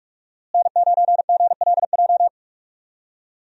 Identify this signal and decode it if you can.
Morse code, heard as a keyed tone, about 46 wpm, 700 Hz. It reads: N9GLJ
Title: USA Call Signs - 5 Characters - Repeated 3x - 45wpm